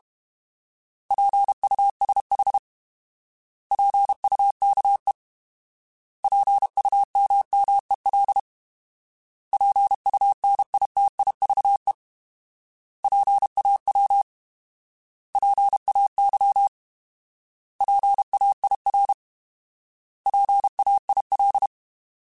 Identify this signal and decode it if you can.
Morse code, heard as a keyed tone, about 32 wpm, 800 Hz. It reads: PUSH PUKE PUMMEL PUNITIVE PAW PAY PAIR PAIL